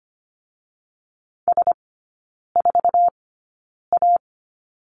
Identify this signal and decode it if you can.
Morse code, heard as a keyed tone, about 25 wpm, spaced as for 9 wpm, 690 Hz. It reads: S4A
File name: Lesson 14e - Number 2 - Three Characters 25wpm